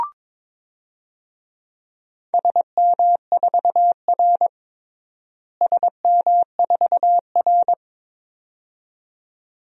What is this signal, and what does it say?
Morse code, heard as a keyed tone, about 22 wpm, 700 Hz. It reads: SM4R SM4R
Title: DXCC-100 Call Signs - 4 Characters - Repeated 2x - 22wpm